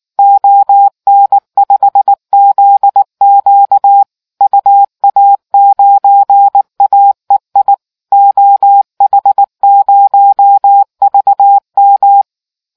Morse code, heard as a keyed tone, about 19 wpm, 800 Hz. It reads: ON5ZQ UA9AEI OH0VM